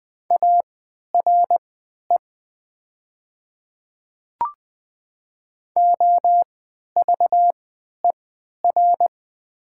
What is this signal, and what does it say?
Morse code, heard as a keyed tone, about 20 wpm, spaced as for 11 wpm, 700 Hz. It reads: ARE OVER
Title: 3x Character Spacing - Top 200 Words - 20wpm